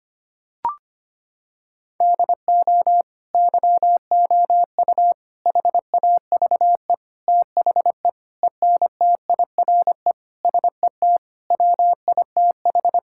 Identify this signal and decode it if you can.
Morse code, heard as a keyed tone, about 25 wpm, 700 Hz. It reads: DO YOU HAVE THE ENTIRE SET WITH
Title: Sentences from Top 1000 Words - Review Part 5 - 25wpm